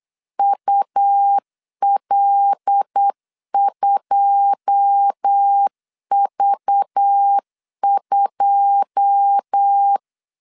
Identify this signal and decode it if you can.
Morse code, heard as a keyed tone, about 8 wpm, 800 Hz. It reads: UL2V2
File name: LEZIONI / 9 2 8wpm 3437